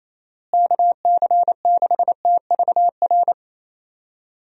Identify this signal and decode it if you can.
Morse code, heard as a keyed tone, about 28 wpm, 700 Hz. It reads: KC6TVR
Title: USA Call Signs - 6 Characters - Repeated 3x - 28wpm